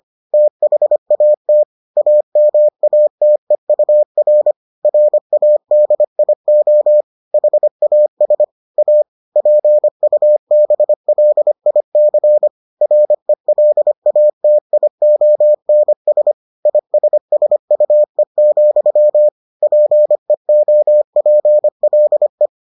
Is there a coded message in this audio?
Yes: Morse code from 0.3 to 22.5 s, heard as a keyed tone, about 25 wpm, 600 Hz. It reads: THAT AMATEUR RADIO HAS A PUBLIC RELATIONS ISSUE, PEOPLE